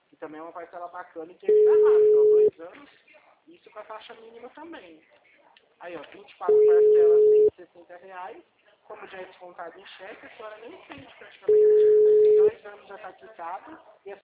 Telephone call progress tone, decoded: ringback tone

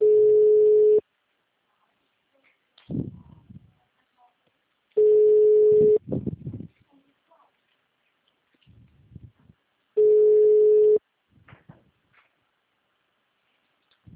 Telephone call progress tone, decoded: ringback tone

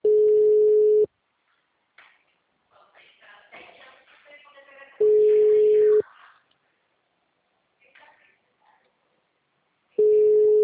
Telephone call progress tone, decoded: ringback tone

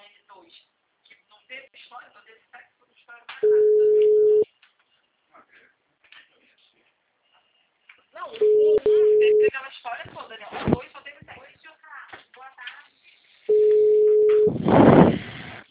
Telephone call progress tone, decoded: ringback tone

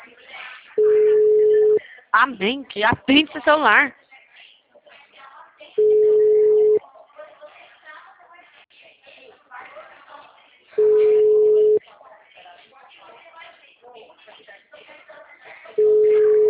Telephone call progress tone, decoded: ringback tone

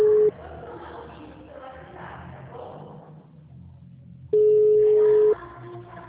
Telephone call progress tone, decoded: ringback tone

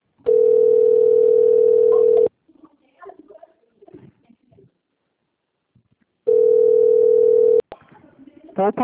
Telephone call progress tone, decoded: ringback tone

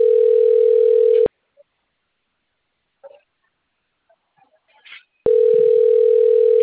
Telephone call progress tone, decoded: ringback tone

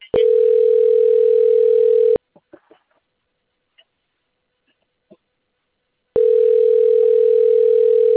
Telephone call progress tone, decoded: ringback tone

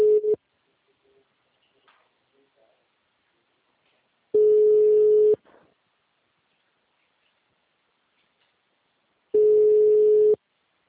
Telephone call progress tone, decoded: ringback tone

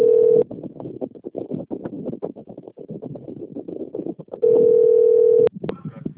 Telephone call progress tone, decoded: ringback tone

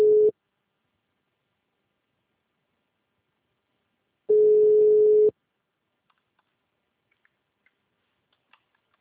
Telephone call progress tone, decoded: ringback tone

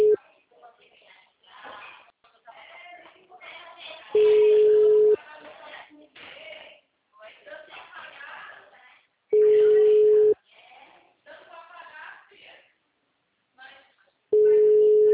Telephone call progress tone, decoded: ringback tone

